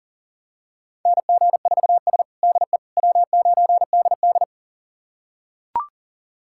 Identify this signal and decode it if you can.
Morse code, heard as a keyed tone, about 37 wpm, 700 Hz. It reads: NG4S DE W9DD
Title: USA Callsign DE Callsign - 4 Characters - 40wpm